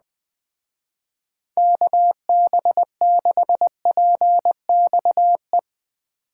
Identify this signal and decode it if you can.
Morse code, heard as a keyed tone, about 21 wpm, 700 Hz. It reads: KB6PXE